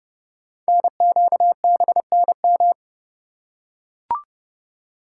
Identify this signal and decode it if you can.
Morse code, heard as a keyed tone, about 30 wpm, 700 Hz. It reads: NQBNM
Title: Code Groups - 5 Letters - 30wpm